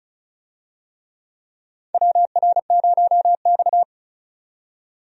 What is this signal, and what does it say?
Morse code, heard as a keyed tone, about 34 wpm, 700 Hz. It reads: WR0X
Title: USA Call Signs - 4 Characters - 35wpm